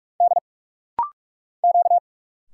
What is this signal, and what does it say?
Morse code, heard as a keyed tone, about 44 wpm, 700 Hz. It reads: D Q
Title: Single Letter-Number - Rapid-Fire 45wpm